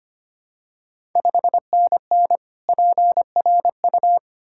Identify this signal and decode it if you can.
Morse code, heard as a keyed tone, about 25 wpm, 700 Hz. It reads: 5NN PRU
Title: Ontario QSO Party Exchange 25wpm